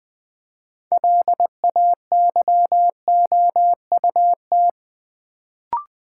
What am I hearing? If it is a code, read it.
Morse code, heard as a keyed tone, about 20 wpm, 690 Hz. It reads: LAYOUT
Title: Build A Word - 2 Letters At A Time - 6-Letter Words - 20wpm